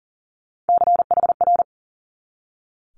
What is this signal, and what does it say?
Morse code, heard as a keyed tone, about 39 wpm, 700 Hz. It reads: CHR
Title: Multi-letter Phonemes 40wpm